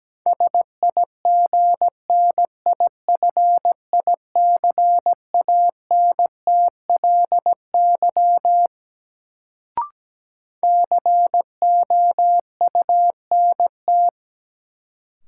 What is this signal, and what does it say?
Morse code, heard as a keyed tone, about 17 wpm, 700 Hz. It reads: SIGNIFICANTLY COUNT
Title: Sentences from Top 1500 Words - New Word Drill Alt - 17wpm